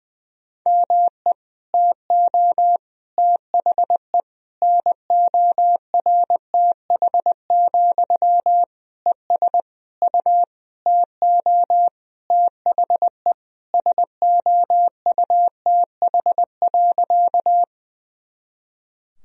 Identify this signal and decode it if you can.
Morse code, heard as a keyed tone, about 20 wpm, 700 Hz. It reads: ME TO THE NORTH, ES U TO THE SOUTH.
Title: Sentences from Top 400 Words and Abbreviations 20wpm